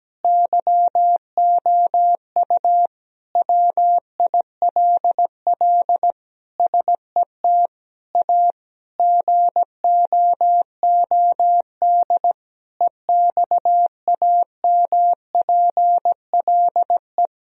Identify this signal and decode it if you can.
Morse code, heard as a keyed tone, about 17 wpm, 700 Hz. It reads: YOU WILL SET A GOOD EXAMPLE
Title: Sentences from Top 1000 Words - Review Part 1 - 17wpm